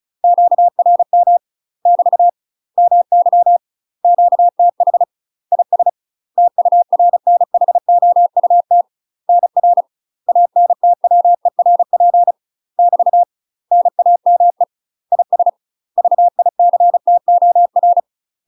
Morse code, heard as a keyed tone, about 36 wpm, 700 Hz. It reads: QRM <BT> MY QTH IS TURNHOUT NR ANTWERP <BT> NAME IS VICTOR
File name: UK MASTER QSOs TRAINING / 35WPM / UK MASTER QSOs 0010